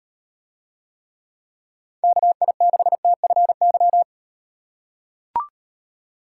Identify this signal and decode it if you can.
Morse code, heard as a keyed tone, about 38 wpm, 700 Hz. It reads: KI6TFY